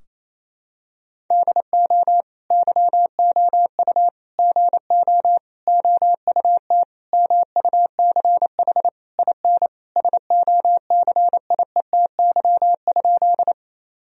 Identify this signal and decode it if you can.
Morse code, heard as a keyed tone, about 27 wpm, 700 Hz. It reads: DO YOU GO OUT MUCH IN SOCIETY?
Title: Sentences from Top 1000 Words - Review Part 4 - 28wpm